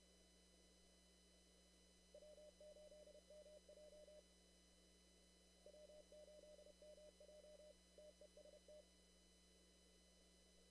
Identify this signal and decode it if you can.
Morse code, heard as a keyed tone, about 30 wpm, 600 Hz. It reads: W8MJ W8MJ TEST
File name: CW / kt5j wpx cw-op-k5pi